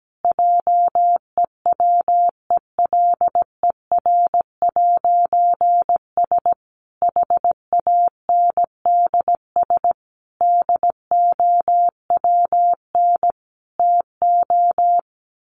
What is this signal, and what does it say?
Morse code, heard as a keyed tone, about 17 wpm, 700 Hz. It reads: JEWELER'S HANDS DOWN TO
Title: A Sound Of Thunder 17wpm